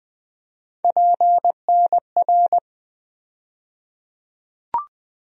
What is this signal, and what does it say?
Morse code, heard as a keyed tone, about 20 wpm, 700 Hz. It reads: PNR
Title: USA Call Sign Postfixes - 3 Letters 20wpm